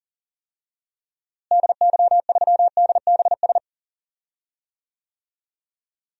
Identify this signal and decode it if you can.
Morse code, heard as a keyed tone, about 40 wpm, 700 Hz. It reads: DY3DBS
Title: DXCC-100 Call Signs - 6 Characters - Repeated 2x - 40wpm